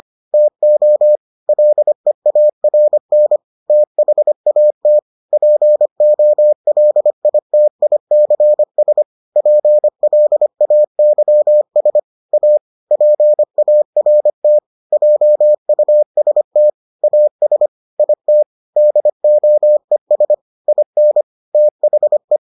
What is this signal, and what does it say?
Morse code, heard as a keyed tone, about 25 wpm, 600 Hz. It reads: TO LEARN THAT POLITICS PLAYS A PART JUST AS IT DOES IN THE